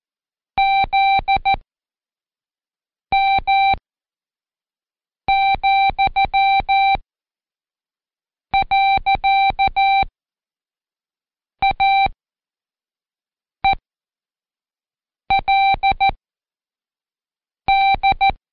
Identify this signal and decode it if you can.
Morse code, heard as a keyed tone, about 14 wpm, 780 Hz. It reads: Z M , . A E L D
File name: Code Quick / AudioDownld1 / Seg74-Day8-Pract 1234